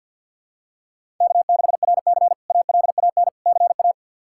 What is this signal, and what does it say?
Morse code, heard as a keyed tone, about 51 wpm, 700 Hz. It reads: K6RC ALAN CA
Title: K1USN SST QSO Practice - 50wpm